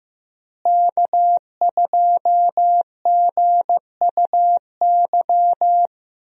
Morse code, heard as a keyed tone, about 15 wpm, 700 Hz. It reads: K2GUY
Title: USA Call Signs - 5 Characters - Repeated 3x - 15wpm